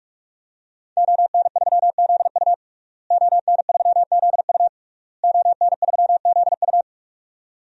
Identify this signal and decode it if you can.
Morse code, heard as a keyed tone, about 42 wpm, 700 Hz. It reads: ON3ZU ON3ZU ON3ZU
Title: DXCC-100 Call Signs - 5 Characters - Repeated 3x - 45wpm